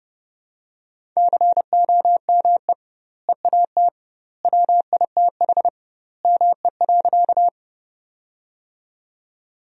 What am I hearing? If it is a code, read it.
Morse code, heard as a keyed tone, about 30 wpm, 700 Hz. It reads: COME EAT WITH ME.
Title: 2x Word-Spacing - Sentences from Top 300 Words - 30wpm